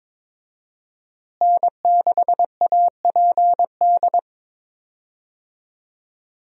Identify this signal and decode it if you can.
Morse code, heard as a keyed tone, about 22 wpm, 700 Hz. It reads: N6APD
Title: USA Call Signs - 5 Characters - Repeated 2x - 22wpm